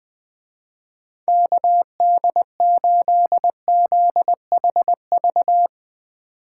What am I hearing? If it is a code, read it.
Morse code, heard as a keyed tone, about 22 wpm, 700 Hz. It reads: KD8ZHV
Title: Speed Racing - USA 2x3 Call Signs 20wpm